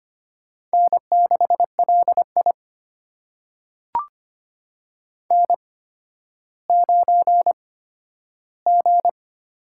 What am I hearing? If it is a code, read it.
Morse code, heard as a keyed tone, about 25 wpm, spaced as for 12 wpm, 700 Hz. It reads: N6LI N9G
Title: USA Call Signs - 8x Character Spacing - 25wpm